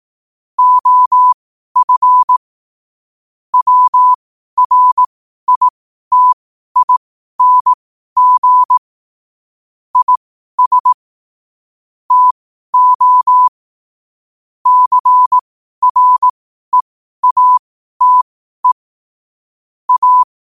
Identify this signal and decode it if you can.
Morse code, heard as a keyed tone, about 18 wpm, spaced as for 12 wpm, 1000 Hz. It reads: OF WRITING IS TO CREATE A
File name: qotdfiles / 12wpm